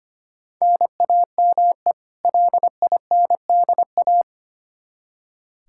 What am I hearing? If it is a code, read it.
Morse code, heard as a keyed tone, about 25 wpm, 700 Hz. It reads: NAME LINDA
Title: Practice QSO - Exchange 1 - Beginning - 25wpm